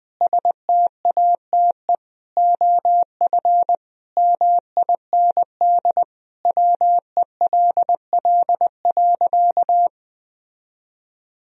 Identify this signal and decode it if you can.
Morse code, heard as a keyed tone, about 20 wpm, 700 Hz. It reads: STATE OF MIND WELL.
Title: Sentences from Top 1000 Words - Review Part 2 - 20wpm